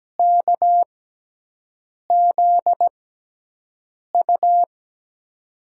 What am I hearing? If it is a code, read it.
Morse code, heard as a keyed tone, about 17 wpm, spaced as for 6 wpm, 700 Hz. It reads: KZU
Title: USA Call Signs - 6x Character Spacing - 17wpm